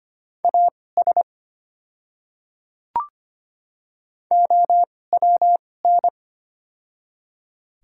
Morse code, heard as a keyed tone, about 25 wpm, spaced as for 18 wpm, 700 Hz. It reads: AS OWN
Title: 2x Character Spacing - Top 300 Words - 25wpm